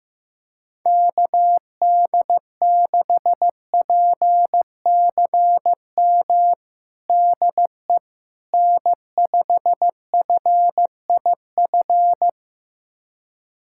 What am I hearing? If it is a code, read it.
Morse code, heard as a keyed tone, about 16 wpm, 700 Hz. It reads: KD6PCM DE N5FIF
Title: USA Callsign DE Callsign 15wpm